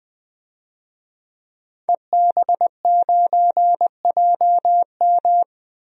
Morse code, heard as a keyed tone, about 20 wpm, 700 Hz. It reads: EB9JM
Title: DXCC-100 Call Signs - 5 Characters - 20wpm